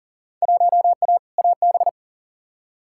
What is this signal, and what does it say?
Morse code, heard as a keyed tone, about 40 wpm, 700 Hz. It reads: 1A AB
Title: Field Day Exchanges - 40wpm